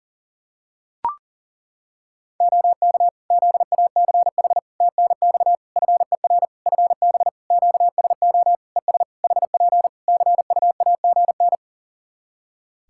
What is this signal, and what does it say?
Morse code, heard as a keyed tone, about 40 wpm, 700 Hz. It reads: OK ZACH TNX FER FB QSO ES HP CUAGN
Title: K1USN SST QSO Practice 40wpm